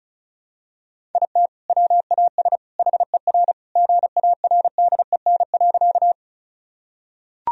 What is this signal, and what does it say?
Morse code, heard as a keyed tone, about 36 wpm, 700 Hz. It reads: IT WAS HER GARDEN.
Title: Sentences from Top 1000 Words - Review Part 4 - 35wpm